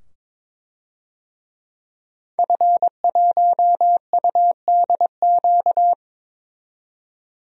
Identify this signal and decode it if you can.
Morse code, heard as a keyed tone, about 21 wpm, 700 Hz. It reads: F1UDQ
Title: DXCC-100 Call Signs - 5 Characters - 22wpm